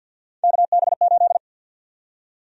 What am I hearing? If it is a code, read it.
Morse code, heard as a keyed tone, about 53 wpm, 700 Hz. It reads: KB8
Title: USA Call Sign Prefixes - 50wpm